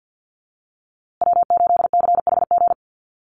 Morse code, heard as a keyed tone, about 50 wpm, 700 Hz. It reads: W8CHG
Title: USA Call Signs Repeated 3x 50wpm